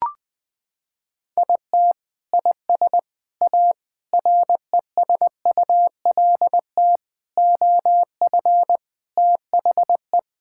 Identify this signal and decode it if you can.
Morse code, heard as a keyed tone, about 19 wpm, 700 Hz. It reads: IT IS A RESULT OF THE
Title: Sentences with 4-Word Phrases - 20wpm